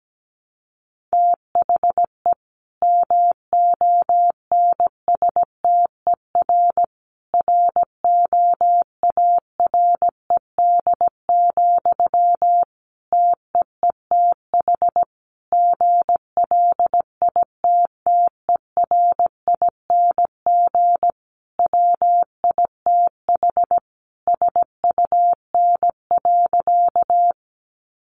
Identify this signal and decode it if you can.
Morse code, heard as a keyed tone, about 17 wpm, 700 Hz. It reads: THE MONSTER ROARED, TEETH GLITTERING WITH SUN.
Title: A Sound Of Thunder 17wpm